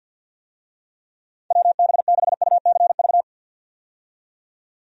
Morse code, heard as a keyed tone, about 52 wpm, 700 Hz. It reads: WB6UCV
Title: USA Call Signs - 6 Characters - 50wpm